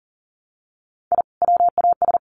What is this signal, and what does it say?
Morse code, heard as a keyed tone, about 40 wpm, 700 Hz. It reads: I WAS
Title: Sets of 5 Words 40wpm